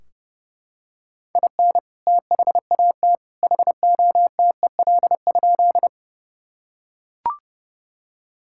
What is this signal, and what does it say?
Morse code, heard as a keyed tone, about 32 wpm, 700 Hz. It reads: IN THAT HOTEL?
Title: Sentences from Top 1600 Words 30wpm